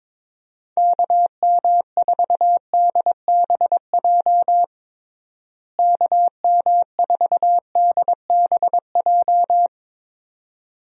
Morse code, heard as a keyed tone, about 22 wpm, 700 Hz. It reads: KM4DBJ KM4DBJ
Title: USA Call Signs - 6 Characters - Repeated 2x - 22wpm